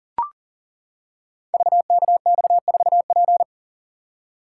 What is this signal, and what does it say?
Morse code, heard as a keyed tone, about 40 wpm, 700 Hz. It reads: VKX4P
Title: Code Groups - 5 Letter:Numbers - 40wpm